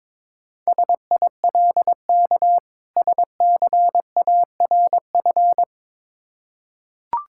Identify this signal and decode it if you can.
Morse code, heard as a keyed tone, about 22 wpm, 700 Hz. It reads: SILK SCARF
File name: Difficult Words with Hints - 22wpm